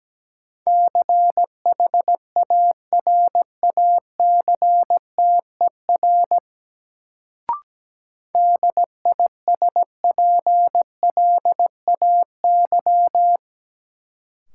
Morse code, heard as a keyed tone, about 17 wpm, 700 Hz. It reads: CHARACTER DISPLAY
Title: Sentences from Top 1200 Words - New Word Drill - 17wpm